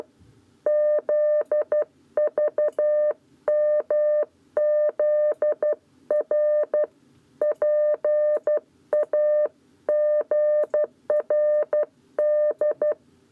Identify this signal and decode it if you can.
Morse code, heard as a keyed tone, about 12 wpm, 600 Hz. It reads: ZVMZR PAGRD
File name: 7b lezione cw 1